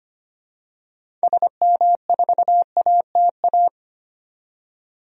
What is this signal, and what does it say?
Morse code, heard as a keyed tone, about 25 wpm, 700 Hz. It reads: SM4ATA
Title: DXCC-100 Call Signs - 6 Characters - 25wpm